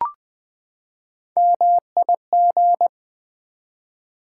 Morse code, heard as a keyed tone, about 20 wpm, 700 Hz. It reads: MIG
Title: Build A Word - 3 Letters At A Time - 9-Letter Words - 20wpm